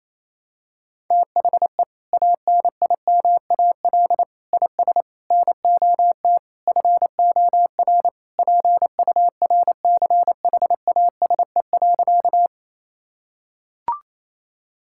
Morse code, heard as a keyed tone, about 28 wpm, 700 Hz. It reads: THE ANIMAL IS NOT FOR PURCHASE.
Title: Sentences from Top 1000 Words - Review Part 5 - 28wpm